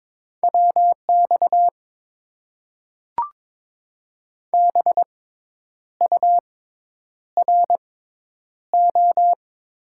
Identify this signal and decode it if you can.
Morse code, heard as a keyed tone, about 22 wpm, spaced as for 9 wpm, 700 Hz. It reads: WX BURO
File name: Top 90 QSO Elements - 6x Character Spacing - 22wpm